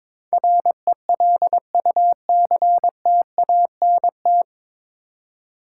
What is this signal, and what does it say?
Morse code, heard as a keyed tone, about 22 wpm, 700 Hz. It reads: RELUCTANT
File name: 9 Letter Words - First 3 Letters Given - 22wpm